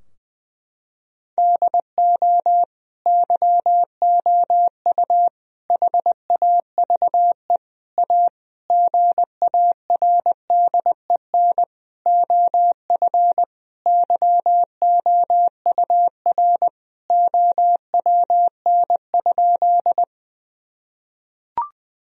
Morse code, heard as a keyed tone, about 20 wpm, 700 Hz. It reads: DO YOU HAVE A GARDEN OF YOUR OWN?
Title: Sentences from Top 1000 Words - Review Part 4 - 20wpm